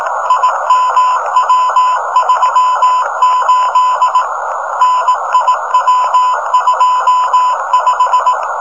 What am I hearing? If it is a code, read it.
Morse code, heard as a keyed tone, about 18 wpm, 1000 Hz, with noise. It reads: IMW38 NIW25
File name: rs-15 cw telemetry